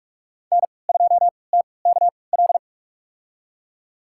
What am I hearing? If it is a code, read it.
Morse code, heard as a keyed tone, about 46 wpm, spaced as for 26 wpm, 700 Hz. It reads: N2TKL
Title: USA Call Signs - 5 Characters - 3x Character Spacing - 45wpm